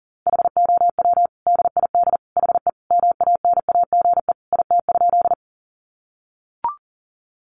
Morse code, heard as a keyed tone, about 40 wpm, 700 Hz. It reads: HOW DID HE MANAGE IT?
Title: Sentences from Top 1400 Words 40wpm